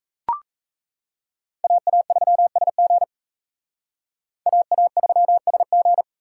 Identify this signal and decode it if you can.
Morse code, heard as a keyed tone, about 34 wpm, 700 Hz. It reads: AA3SG AA3SG
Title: Speed Racing - USA 2x2 Call Signs 28wpm